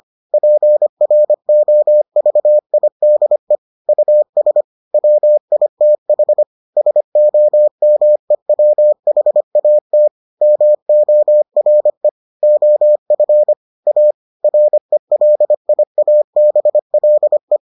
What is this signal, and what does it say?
Morse code, heard as a keyed tone, about 25 wpm, 600 Hz. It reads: PROVIDE US WITH SOMEWHAT MORE OF A RELIABLE